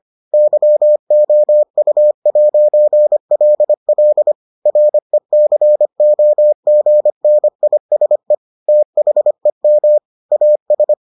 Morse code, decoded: YOU'LL RECOGNISE THEM AS

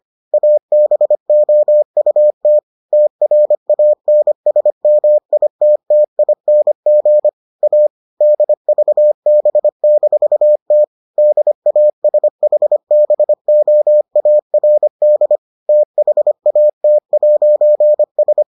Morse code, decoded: ABOUT TRANSMITTING A DVB-T DASHBOARD THAT'S

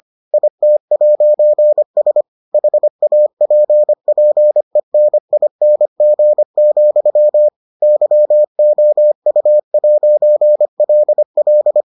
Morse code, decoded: IT'S HAPPENING, YOU'LL

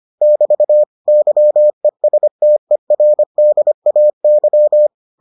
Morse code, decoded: <BT> YESTERDAY

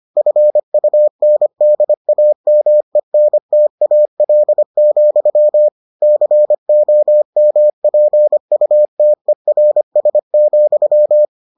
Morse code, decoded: FUNDAMENTAL, COMPUTERS,

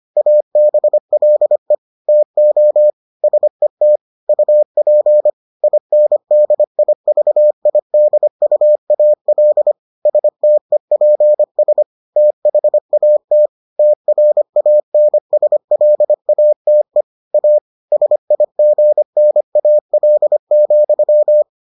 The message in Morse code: ABLE TO SET UP INDIVIDUAL STEPS THAT TRANSLATE A SIGNAL,

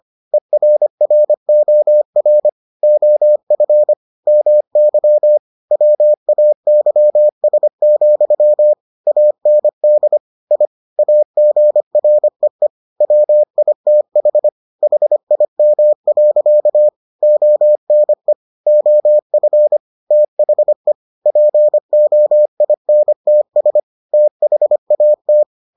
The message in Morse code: ERROR OF MY WAYS, AND I AGREE WITH HIM. ONE OF THE POINTS THAT

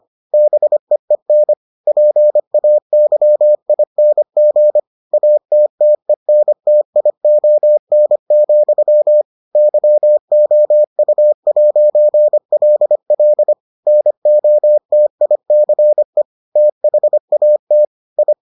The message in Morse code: BEEN PAYING ATTENTION, YOU'LL NOTICE THAT I

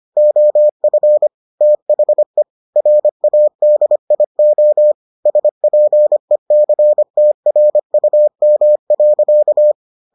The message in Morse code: OF THE RADIO SPECTRUM.

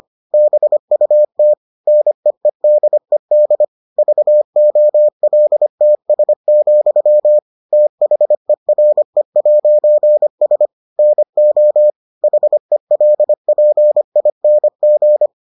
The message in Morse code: BUT NEEDED VOLTS, THERE'S NO HELPING

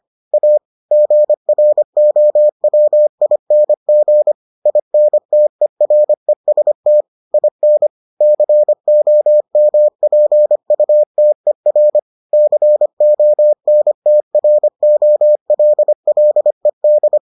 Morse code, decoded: A GROWING INTEREST IN COMPUTER CONTROLLED

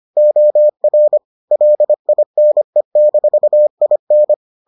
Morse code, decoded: OR LINE-IN

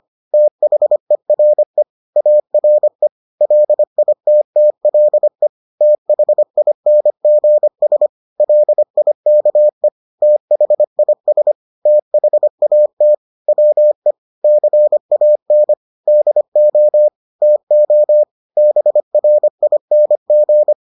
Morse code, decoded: THERE ARE LITTLE THINGS LIKE THIS THAT WE CAN DO TO BRING

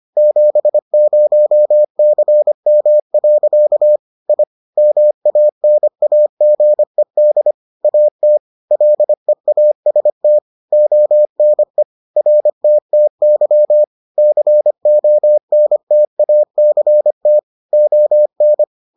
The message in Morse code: 70CM. I MANAGED AT LEAST ONE RTTY CONTACT ON

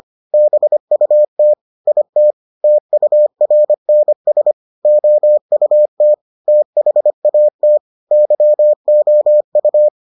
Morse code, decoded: BUT IT TURNS OUT THAT YOU

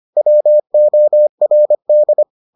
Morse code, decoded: WORD

